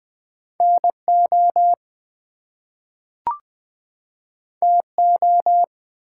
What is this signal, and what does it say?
Morse code, heard as a keyed tone, about 20 wpm, 700 Hz. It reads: NO TO
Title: Two Letter Words - 20wpm